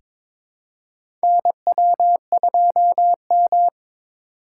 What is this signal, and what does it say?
Morse code, heard as a keyed tone, about 21 wpm, 700 Hz. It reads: NW2M
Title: USA Call Signs - 6x Character Spacing - 22wpm